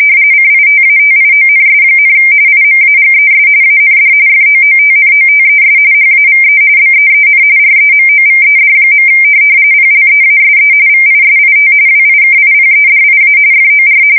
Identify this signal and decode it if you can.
RTTY, about 45.45 baud. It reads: CITATIONS, IMAGES AND FOOTNOTES HAVE BEEN REMOVED. PLEASE VISIT WIKIPEDIA FOR
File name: fsk170 / USS-Missouri